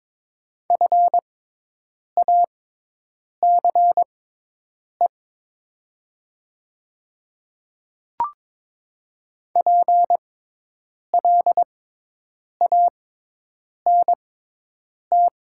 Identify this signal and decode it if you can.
Morse code, heard as a keyed tone, about 22 wpm, spaced as for 8 wpm, 700 Hz. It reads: FACE PLANT
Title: 6x Character Spacing - Top 300 Words - 22wpm